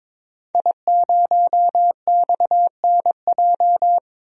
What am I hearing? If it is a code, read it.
Morse code, heard as a keyed tone, about 22 wpm, 700 Hz. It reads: I0XNJ
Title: DXCC-100 Call Signs - 5 Characters - Repeated 2x - 22wpm